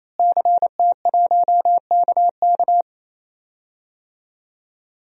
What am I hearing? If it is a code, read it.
Morse code, heard as a keyed tone, about 28 wpm, 700 Hz. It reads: CT1KK
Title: DXCC-100 Call Signs - 5 Characters - Repeated 3x - 28wpm